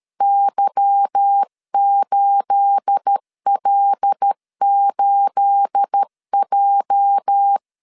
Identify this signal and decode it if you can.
Morse code, heard as a keyed tone, about 13 wpm, 800 Hz. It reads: Y8L8J
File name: LEZIONI / 9 28 12wpm 2351